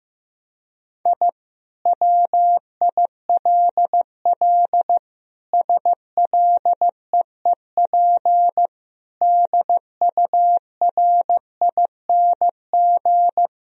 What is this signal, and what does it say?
Morse code, heard as a keyed tone, about 16 wpm, 700 Hz. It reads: I WILL SLEEP DURING
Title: Sentences from Top 1000 Words - Review Part 2 - 15wpm